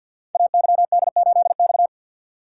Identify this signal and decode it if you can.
Morse code, heard as a keyed tone, about 52 wpm, 700 Hz. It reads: AYD8X
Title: Code Groups - 5 Letter:Numbers - 50wpm